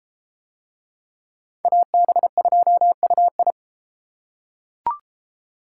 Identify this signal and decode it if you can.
Morse code, heard as a keyed tone, about 30 wpm, 700 Hz. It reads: AB2UI